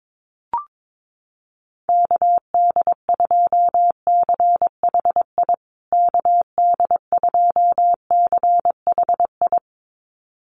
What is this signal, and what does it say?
Morse code, heard as a keyed tone, about 22 wpm, 700 Hz. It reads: KD2CHI KD2CHI
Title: USA Call Signs Repeated 2x 22wpm